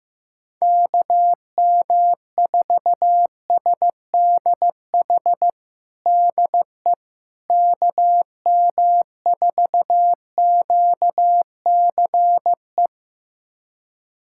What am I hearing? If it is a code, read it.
Morse code, heard as a keyed tone, about 16 wpm, 700 Hz. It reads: KM4SDH DE KM4QCE
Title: USA Callsign DE Callsign 15wpm